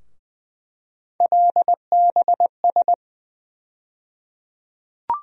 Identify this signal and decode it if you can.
Morse code, heard as a keyed tone, about 20 wpm, 700 Hz. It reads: LBS
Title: USA Call Sign Postfixes - 3 Letters 20wpm